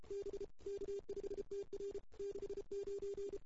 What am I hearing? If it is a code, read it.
Morse code, heard as a keyed tone, about 33 wpm, 410 Hz. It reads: B K5TR 69
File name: CW / 2008-ss-cw-k5tr-op-n5rz -00:00